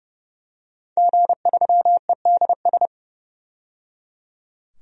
Morse code, heard as a keyed tone, about 29 wpm, 700 Hz. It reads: G3EDS
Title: DXCC-100 Call Signs - 5 Characters - 30wpm